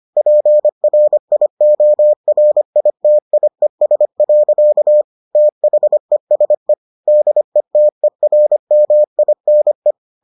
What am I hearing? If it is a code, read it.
Morse code, heard as a keyed tone, about 25 wpm, 600 Hz. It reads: PRIORITIES. THESE DETERMINE